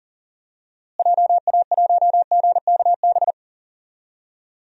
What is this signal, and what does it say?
Morse code, heard as a keyed tone, about 40 wpm, 700 Hz. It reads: JA1GKB
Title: DXCC-100 Call Signs - 6 Characters - Repeated 2x - 40wpm